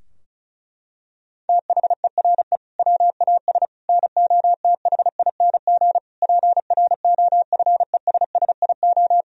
Morse code, heard as a keyed tone, about 36 wpm, 700 Hz. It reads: THERE WAS NOTHING PROFESSIO
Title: Sentences from Top 1000 Words - Review Part 4 - 35wpm